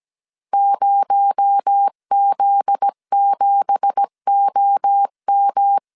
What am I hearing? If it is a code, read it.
Morse code, heard as a keyed tone, about 17 wpm, 800 Hz. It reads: 0Z7OM
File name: 1500C S B 16wpm